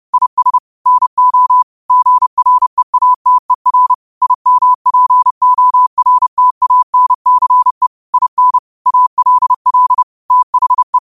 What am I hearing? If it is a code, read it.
Morse code, heard as a keyed tone, about 30 wpm, 1000 Hz. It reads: IS NO GREATER IMPORTANCE IN ALL THE
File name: qotdfiles / 30wpm